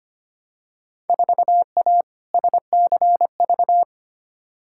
Morse code, heard as a keyed tone, about 25 wpm, 700 Hz. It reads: 4A SCV
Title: Field Day Exchanges Encore - 25wpm